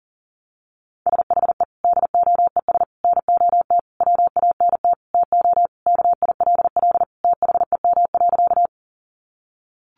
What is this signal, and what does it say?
Morse code, heard as a keyed tone, about 40 wpm, 700 Hz. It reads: SHE DOES NOT WANT TO KILL THEM.
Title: Easy Phrases 40wpm